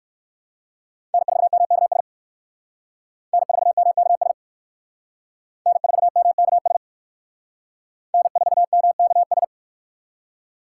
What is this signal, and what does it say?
Morse code, heard as a keyed tone, about 59 wpm, 700 Hz. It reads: N4MKS N4MKS N4MKS N4MKS